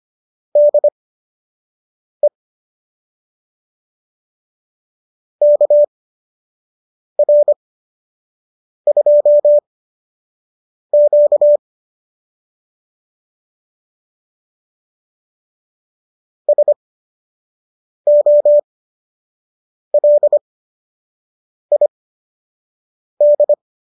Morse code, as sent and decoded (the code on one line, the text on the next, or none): -.. / -.- .-. ..--- --.- / ... --- .-.. .. -..
DE KR2Q SOLID